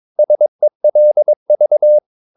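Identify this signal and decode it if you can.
Morse code, heard as a keyed tone, about 22 wpm, 600 Hz. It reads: SELV
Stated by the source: Morse practice of short Danish words at 22 wpm, from CW med Gnister.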